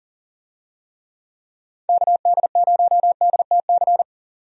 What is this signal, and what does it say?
Morse code, heard as a keyed tone, about 40 wpm, 700 Hz. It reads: KD0DTC
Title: USA Call Signs - 6 Characters - 40wpm